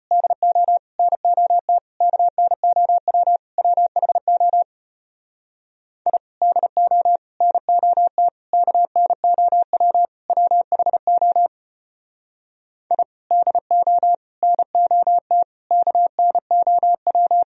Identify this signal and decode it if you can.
Morse code, heard as a keyed tone, about 33 wpm, 700 Hz. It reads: DO NOT KNOW WHO I DO NOT KNOW WHO I DO NOT KNOW